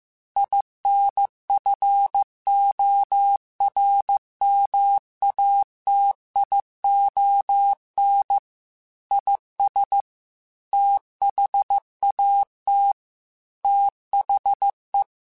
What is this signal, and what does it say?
Morse code, heard as a keyed tone, about 15 wpm, 800 Hz. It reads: INFORMATION IS THAT THE
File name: Technical Subjects / Technical Subjects-mp3s / Sw Pwr Sup @13